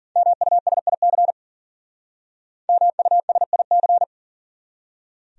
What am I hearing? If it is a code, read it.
Morse code, heard as a keyed tone, about 50 wpm, 700 Hz. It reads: MUSIC MUSIC
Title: Speed-Racing Top 500 Words 40wpm